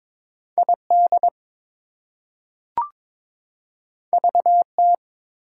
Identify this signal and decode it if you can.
Morse code, heard as a keyed tone, about 22 wpm, 700 Hz. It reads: ID VT
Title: US States and Canadian Provinces 22wpm